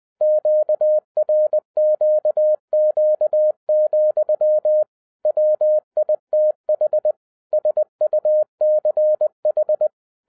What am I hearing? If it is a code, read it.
Morse code, heard as a keyed tone, about 20 wpm, 610 Hz. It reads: QRQQ, WITH SUCH
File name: 2011-10-mp3 / n13-TIPS FROM MY 34 YEARS OF EXPERIENCE 20wpm